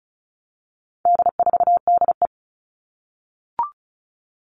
Morse code, heard as a keyed tone, about 34 wpm, 700 Hz. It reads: D4DE
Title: All Cut Numbers - 4-Digit 35wpm